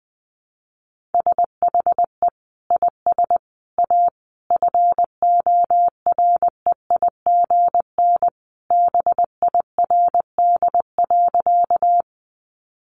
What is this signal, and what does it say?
Morse code, heard as a keyed tone, about 20 wpm, 700 Hz. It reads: SHE IS A FOREIGN BIRD.
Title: Sentences from Top 1400 Words 20wpm